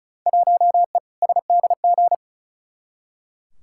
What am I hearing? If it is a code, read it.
Morse code, heard as a keyed tone, about 36 wpm, 700 Hz. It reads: 1E SDG
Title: Field Day Exchanges - 35wpm